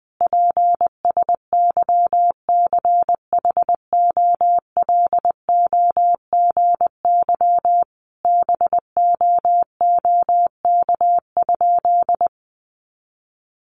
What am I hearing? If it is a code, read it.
Morse code, heard as a keyed tone, about 20 wpm, 700 Hz. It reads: PSYCHOLOGY BOOK?
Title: Sentences from Top 1400 Words 20wpm